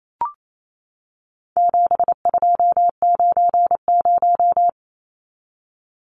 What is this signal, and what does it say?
Morse code, heard as a keyed tone, about 28 wpm, 700 Hz. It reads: 7290
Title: Numbers - 4-Digit 28wpm